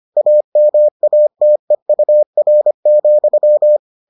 Morse code, heard as a keyed tone, about 25 wpm, 600 Hz. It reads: AMATEUR,